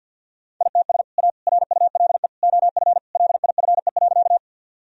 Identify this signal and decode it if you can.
Morse code, heard as a keyed tone, about 50 wpm, 700 Hz. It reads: ITS A RULE OF LIFE.
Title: Sentences from Top 1000 Words - Review Part 2 - 50wpm